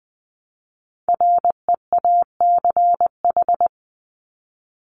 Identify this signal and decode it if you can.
Morse code, heard as a keyed tone, about 20 wpm, 700 Hz. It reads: REACH
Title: Top 500 Words Repeated 20wpm